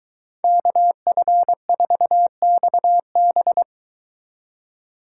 Morse code, heard as a keyed tone, about 23 wpm, 700 Hz. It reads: KF4XB